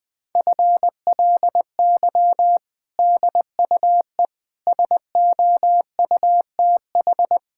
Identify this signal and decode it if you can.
Morse code, heard as a keyed tone, about 20 wpm, 700 Hz. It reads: FLY DUE SOUTH
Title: Sentences from Top 1000 Words - Review Part 6 - 20wpm